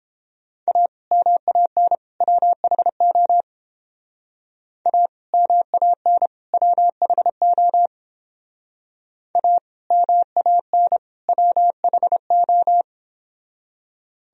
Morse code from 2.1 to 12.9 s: WHO A MAN WHO A MAN WHO

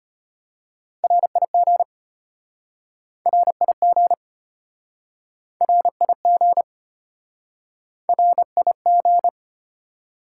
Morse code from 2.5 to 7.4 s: RIG RIG

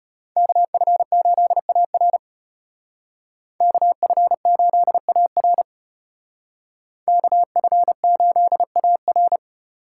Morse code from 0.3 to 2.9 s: KF8AR

